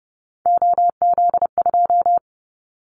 Morse code, heard as a keyed tone, about 30 wpm, 700 Hz. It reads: OZ2
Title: Callsign Prefixes for Top 50 DXCC - 30wpm